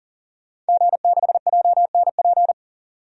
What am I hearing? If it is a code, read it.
Morse code, heard as a keyed tone, about 40 wpm, 700 Hz. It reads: G6JNP
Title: DXCC-100 Call Signs - 5 Characters - 40wpm